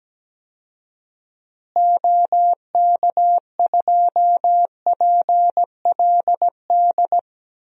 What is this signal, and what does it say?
Morse code, heard as a keyed tone, about 17 wpm, 700 Hz. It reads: OK2PLD
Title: DXCC-100 Call Signs - 6 Characters - Repeated 3x - 17wpm